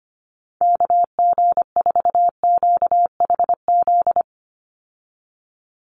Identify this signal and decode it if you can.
Morse code, heard as a keyed tone, about 25 wpm, 700 Hz. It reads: KG4QHZ
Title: USA Call Signs 25wpm